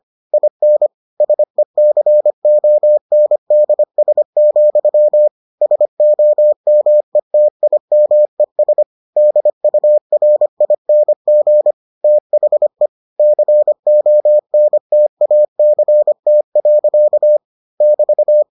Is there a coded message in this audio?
Yes: Morse code from 0.3 to 18.4 s, heard as a keyed tone, about 25 wpm, 600 Hz. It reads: IN SECONDS, SOMETIMES DURING THE CONTACT. <BT>